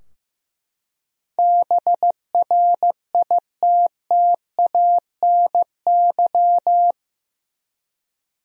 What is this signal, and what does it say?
Morse code, heard as a keyed tone, about 15 wpm, 700 Hz. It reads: BRITTANY
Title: USA Common Names - 15wpm